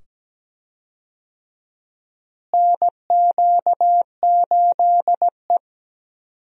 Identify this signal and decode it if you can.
Morse code, heard as a keyed tone, about 17 wpm, 700 Hz. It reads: NQ8E
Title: USA Call Signs - 4 Characters - Repeated 2x - 17wpm